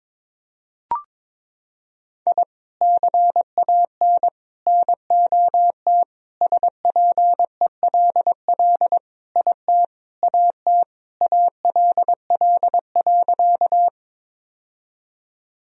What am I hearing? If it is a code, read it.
Morse code, heard as a keyed tone, about 21 wpm, 700 Hz. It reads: I CAN NOT SPELL IT AT ALL.
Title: Sentences from Top 1000 Words - Review Part 1 - 22wpm